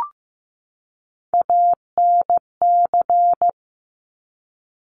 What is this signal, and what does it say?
Morse code, heard as a keyed tone, about 15 wpm, 700 Hz. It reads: ANC
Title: Top Three Letter Combinations in Words 15wpm